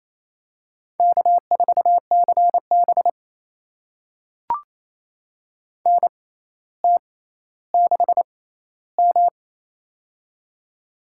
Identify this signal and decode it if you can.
Morse code, heard as a keyed tone, about 28 wpm, spaced as for 14 wpm, 700 Hz. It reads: K4CB NT6M
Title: USA Call Signs - 4 Characters - 6x Character Spacing - 28wpm